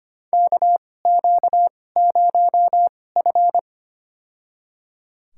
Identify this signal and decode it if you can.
Morse code, heard as a keyed tone, about 25 wpm, spaced as for 18 wpm, 700 Hz. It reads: KQ0F
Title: USA Call Signs - 4 Characters - 2x Character Spacing - 25wpm